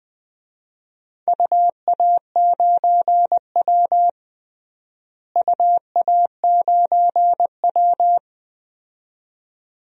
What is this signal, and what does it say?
Morse code, heard as a keyed tone, about 20 wpm, 700 Hz. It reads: UA9W UA9W
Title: DXCC-100 Call Signs - 4 Characters - Repeated 2x - 20wpm